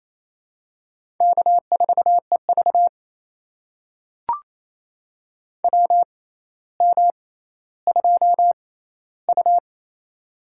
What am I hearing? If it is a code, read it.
Morse code, heard as a keyed tone, about 28 wpm, spaced as for 14 wpm, 700 Hz. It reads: K4EV WM2U
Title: USA Call Signs - 4 Characters - 6x Character Spacing - 28wpm